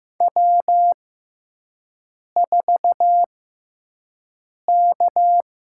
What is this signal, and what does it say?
Morse code, heard as a keyed tone, about 16 wpm, spaced as for 5 wpm, 700 Hz. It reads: W4K
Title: USA Call Signs - 5 Characters - 6x Character Spacing - 15wpm